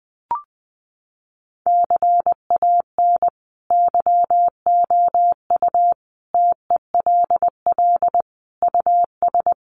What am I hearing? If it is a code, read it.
Morse code, heard as a keyed tone, about 20 wpm, 700 Hz. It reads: CAN YOU TELL US
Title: Sets of 5 Words Encore2 20wpm